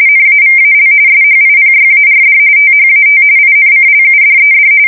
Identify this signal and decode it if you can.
RTTY, about 45.45 baud. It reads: NAVAL SHIPYARD THAT INCLUDED
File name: fsk170 / USS-Missouri